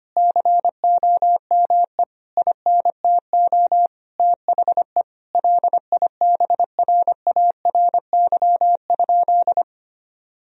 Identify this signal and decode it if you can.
Morse code, heard as a keyed tone, about 25 wpm, 700 Hz. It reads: COME INTO THE LIBRARY?
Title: Sentences from Top 1000 Words - Review Part 4 - 25wpm